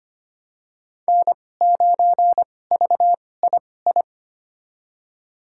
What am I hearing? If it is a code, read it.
Morse code, heard as a keyed tone, about 25 wpm, spaced as for 18 wpm, 700 Hz. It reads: N9VII
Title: USA Call Signs - 2x Character Spacing - 25wpm